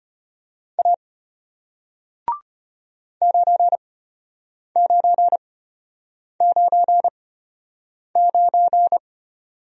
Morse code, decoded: A 9999

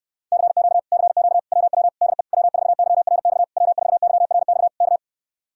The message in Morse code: CQ CQ WW DE W4YRX W4YRX K

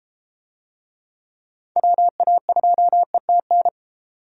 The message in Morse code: WA2ETN

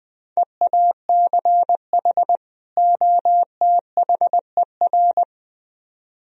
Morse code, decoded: EACH OTHER

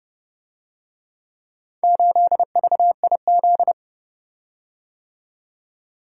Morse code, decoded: 8VIZ

